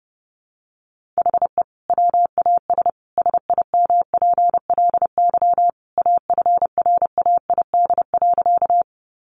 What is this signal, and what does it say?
Morse code, heard as a keyed tone, about 30 wpm, 700 Hz. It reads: HE WAS SIMPLY AFRAID.
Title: Sentences from Top 1500 Words 30wpm